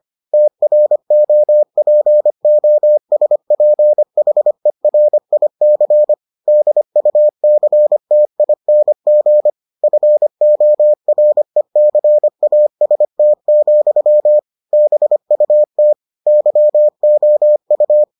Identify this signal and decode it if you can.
Morse code, heard as a keyed tone, about 25 wpm, 600 Hz. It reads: TROPOSPHERIC DUCTING FORECAST, BUT YOU